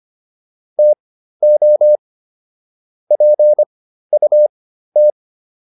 Morse code, heard as a keyed tone, about 25 wpm, spaced as for 13 wpm, 600 Hz. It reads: TO PUT
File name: WD404 13